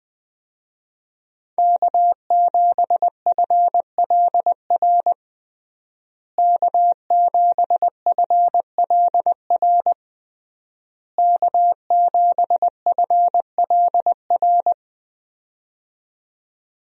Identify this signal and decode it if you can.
Morse code, heard as a keyed tone, about 20 wpm, 700 Hz. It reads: K7FLR K7FLR K7FLR
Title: USA Call Signs - 5 Characters - Repeated 3x - 20wpm